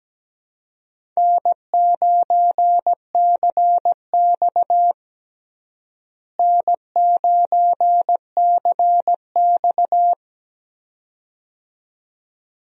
Morse code, heard as a keyed tone, about 17 wpm, 700 Hz. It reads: N9CX N9CX
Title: USA Call Signs - 4 Characters - Repeated 2x - 17wpm